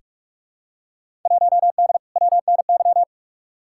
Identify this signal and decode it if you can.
Morse code, heard as a keyed tone, about 43 wpm, 700 Hz. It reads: 1D WNY
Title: Field Day Exchanges - 45wpm